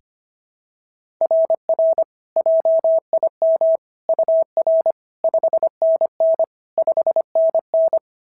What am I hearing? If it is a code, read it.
Morse code, heard as a keyed tone, about 25 wpm, 690 Hz. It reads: RR JIM UR 5NN 5NN
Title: SOTA QSO Practice 25wpm